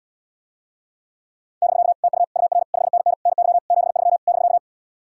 Morse code, heard as a keyed tone, about 80 wpm, 700 Hz. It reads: <BK> TU UR 5NN NY 73 <BK>